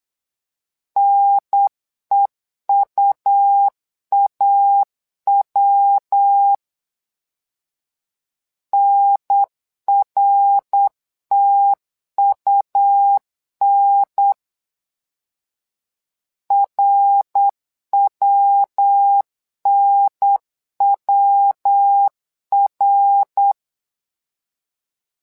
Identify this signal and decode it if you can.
Morse code, heard as a keyed tone, about 9 wpm, 800 Hz. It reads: NEUAW NRTUN RWNWR
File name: LEZIONI / 3 MWRO / 3 MWRO 8wpm 2606